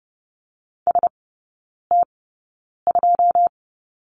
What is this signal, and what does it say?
Morse code, heard as a keyed tone, about 30 wpm, spaced as for 9 wpm, 700 Hz. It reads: ST2
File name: 081 Lesson 14e - Number 2 - Three Characters 30wpm